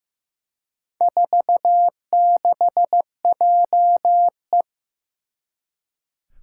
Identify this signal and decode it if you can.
Morse code, heard as a keyed tone, about 15 wpm, 700 Hz. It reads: 46JE
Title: Four Challenging Letter-Number Combinations 15wpm